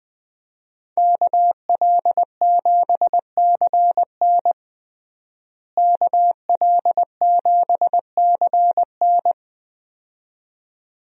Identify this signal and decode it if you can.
Morse code, heard as a keyed tone, about 20 wpm, 700 Hz. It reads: KL7CN KL7CN
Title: USA Call Signs - 5 Characters - Repeated 2x - 20wpm